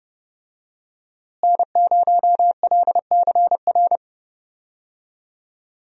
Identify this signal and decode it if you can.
Morse code, heard as a keyed tone, about 31 wpm, 700 Hz. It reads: N0LCR
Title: USA Call Signs - 5 Characters - Repeated 2x - 30wpm